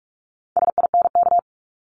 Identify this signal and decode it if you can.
Morse code, heard as a keyed tone, about 46 wpm, 700 Hz. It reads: SINK
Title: Multi-letter Phonemes Repeated 3x 45wpm